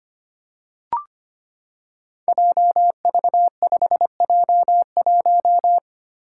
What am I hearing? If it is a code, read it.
Morse code, heard as a keyed tone, about 25 wpm, 700 Hz. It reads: JV5J1
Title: Five Challenging Letter-Number Combinations 25wpm